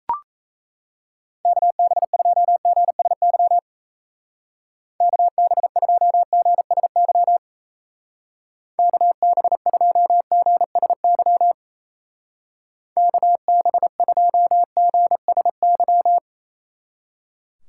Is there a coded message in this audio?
Yes: Morse code from 1.4 to 16.2 s, heard as a keyed tone, about 36 wpm, 700 Hz. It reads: KB2GSY KB2GSY KB2GSY KB2GSY